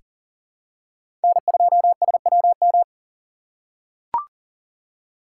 Morse code, heard as a keyed tone, about 40 wpm, 700 Hz. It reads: N2SWM
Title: USA Call Signs - 5 Characters - 2x Character Spacing - 40wpm